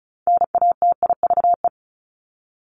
Morse code, heard as a keyed tone, about 34 wpm, 700 Hz. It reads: NATIVE
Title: Top 1751-2000 Words 35wpm